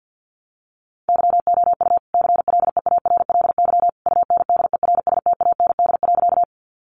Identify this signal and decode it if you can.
Morse code, heard as a keyed tone, about 50 wpm, 700 Hz. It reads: YOU CLEARLY UNDERSTAND.
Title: Sentences from Top 1300 Words 50wpm